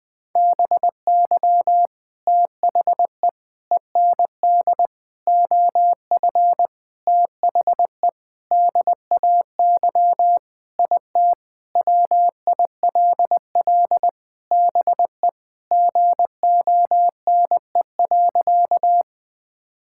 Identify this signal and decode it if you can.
Morse code, heard as a keyed tone, about 21 wpm, 700 Hz. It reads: BY THE END OF THE DAY IT WILL BE GONE.
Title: Sentences with 4-Word Phrases - Spoken Beginning - 20wpm